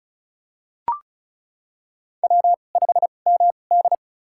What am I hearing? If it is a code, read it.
Morse code, heard as a keyed tone, about 36 wpm, spaced as for 25 wpm, 700 Hz. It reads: W5MD
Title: USA Call Signs - 4 Characters - 2x Character Spacing - 35wpm